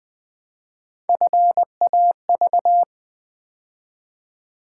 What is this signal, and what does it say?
Morse code, heard as a keyed tone, about 20 wpm, 700 Hz. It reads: FAV
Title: Build A Word - 3 Letters At A Time - 6-Letter Words - 20wpm